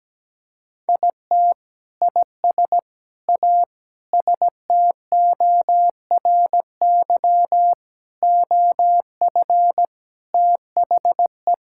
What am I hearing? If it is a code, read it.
Morse code, heard as a keyed tone, about 17 wpm, 700 Hz. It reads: IT IS A STORY OF THE
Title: Sentences from Top 1600 Words 17wpm